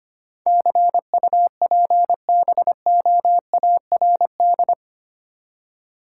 Morse code, heard as a keyed tone, about 25 wpm, 700 Hz. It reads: CUPBOARD
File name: 8 Letter Words - First 3 Letters Given - 25wpm